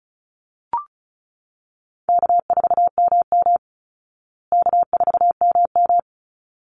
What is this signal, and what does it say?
Morse code, heard as a keyed tone, about 34 wpm, 700 Hz. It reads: K4MM K4MM
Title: Call Signs Repeated 3x - 35wpm